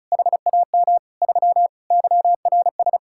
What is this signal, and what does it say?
Morse code, heard as a keyed tone, about 45 wpm, 700 Hz. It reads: HAM 3 YRS